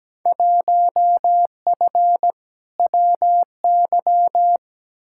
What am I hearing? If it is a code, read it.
Morse code, heard as a keyed tone, about 17 wpm, 700 Hz. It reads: 1F WY
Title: Field Day Exchanges Encore - 17wpm